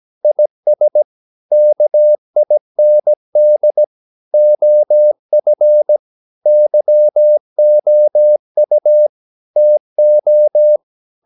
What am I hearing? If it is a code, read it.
Morse code, heard as a keyed tone, about 17 wpm, 600 Hz. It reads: IS KIND OF YOU TO
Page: Romanen Never af Ken Follet på engelsk.